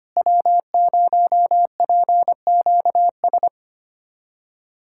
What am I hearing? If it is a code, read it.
Morse code, heard as a keyed tone, about 25 wpm, 700 Hz. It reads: W0PQS
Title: USA Callsign DE Callsign - 5 Characters - 25wpm